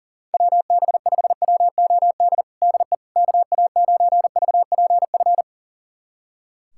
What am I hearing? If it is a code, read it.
Morse code, heard as a keyed tone, about 40 wpm, 690 Hz. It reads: WB5WOD DE KA9VPF
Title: USA Callsign DE Callsign - 6 Characters - 40wpm